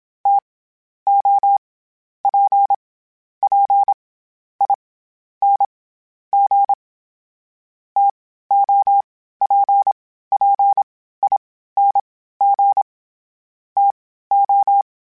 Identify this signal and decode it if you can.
Morse code, heard as a keyed tone, about 26 wpm, 800 Hz. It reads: T O P P I N G T O P P I N G T O
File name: Variable Spacing / Variable Spacing-mp3s / TOs @25